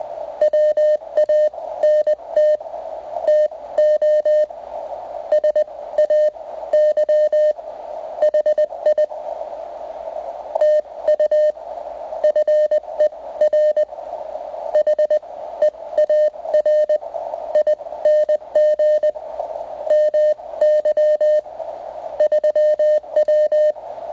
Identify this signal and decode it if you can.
Morse code, heard as a keyed tone, about 20 wpm, spaced as for 16 wpm, 610 Hz, with radio noise. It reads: WANT TO SAY HI TU FER HEARING MY 3W